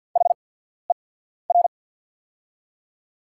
Morse code, heard as a keyed tone, about 48 wpm, spaced as for 14 wpm, 690 Hz. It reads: HER
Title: 8x Character Spacing - Top 500 Words - 50wpm